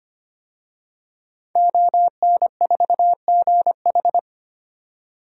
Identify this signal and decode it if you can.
Morse code, heard as a keyed tone, about 25 wpm, 700 Hz. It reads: ON4GH
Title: DXCC-100 Call Signs - 5 Characters - 25wpm